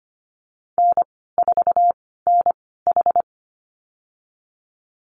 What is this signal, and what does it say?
Morse code, heard as a keyed tone, about 25 wpm, spaced as for 16 wpm, 700 Hz. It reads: N4NH
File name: Lesson 15g - Number 5 - Call Signs 25wpm